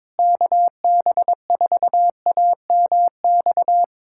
Morse code, heard as a keyed tone, about 22 wpm, 700 Hz. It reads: KB4AMX
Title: USA Callsign DE Callsign - 6 Characters - 22wpm